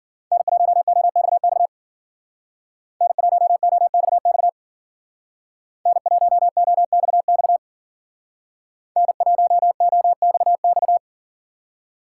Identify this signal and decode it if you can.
Morse code, heard as a keyed tone, about 52 wpm, 700 Hz. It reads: N1OXX N1OXX N1OXX N1OXX